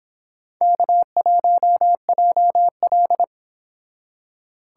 Morse code, heard as a keyed tone, about 25 wpm, 700 Hz. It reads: K1JL